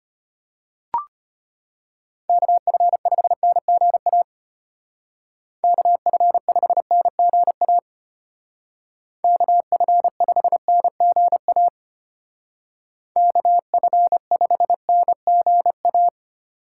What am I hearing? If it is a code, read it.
Morse code, heard as a keyed tone, about 33 wpm, 700 Hz. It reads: KF5NGA KF5NGA KF5NGA KF5NGA